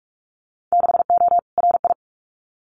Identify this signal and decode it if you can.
Morse code, heard as a keyed tone, about 46 wpm, 700 Hz. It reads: 6O RI
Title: Winter Field Day Exchanges Encore 45wpm